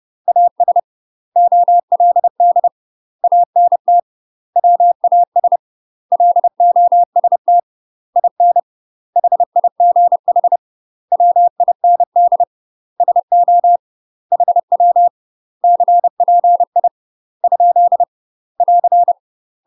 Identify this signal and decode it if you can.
Morse code, heard as a keyed tone, about 30 wpm, 700 Hz. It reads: AS OLD ANT WAS LOST IN HIGH WIND SO HW CPI ? <AR>
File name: UK MASTER QSOs TRAINING / 30WPM / UK MASTER QSOs 0047